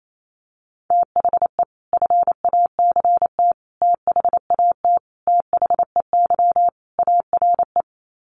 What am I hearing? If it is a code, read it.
Morse code, heard as a keyed tone, about 28 wpm, 700 Hz. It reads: THE FACT THAT THEY ARE
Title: Sets of 5 Words Encore2 28wpm